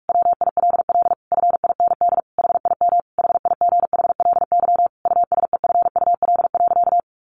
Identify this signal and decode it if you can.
Morse code, heard as a keyed tone, about 44 wpm, 700 Hz. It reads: WILL FIND HIM HIGHLY USEFUL.
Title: Sentences from Top 1100 Words 45wpm